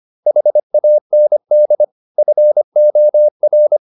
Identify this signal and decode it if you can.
Morse code, heard as a keyed tone, about 25 wpm, 600 Hz. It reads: HAND FOR